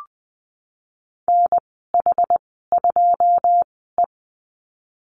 Morse code, heard as a keyed tone, about 20 wpm, spaced as for 15 wpm, 700 Hz. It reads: NH2E
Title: 082 Lesson 14g - Number 2 - Call Signs 20wpm